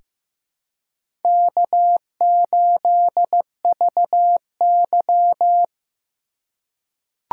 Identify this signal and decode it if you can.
Morse code, heard as a keyed tone, about 15 wpm, 700 Hz. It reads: K8VY
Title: USA Call Signs - 3x Character Spacing - 15wpm